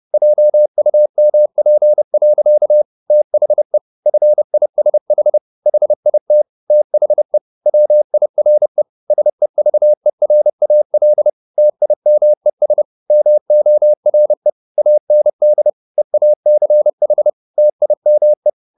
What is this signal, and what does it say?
Morse code, heard as a keyed tone, about 30 wpm, 600 Hz. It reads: JUMP. THE FISH HIT THE WIRE SEVERAL TIMES MORE AND EACH TIME